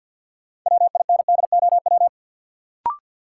Morse code, heard as a keyed tone, about 53 wpm, 700 Hz. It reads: WINDOW
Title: Sentences from Top 1600 Words - New Word Drill - 50wpm